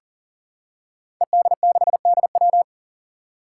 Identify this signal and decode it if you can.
Morse code, heard as a keyed tone, about 40 wpm, 700 Hz. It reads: ED6DW
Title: Five Challenging Letter-Number Combinations 40wpm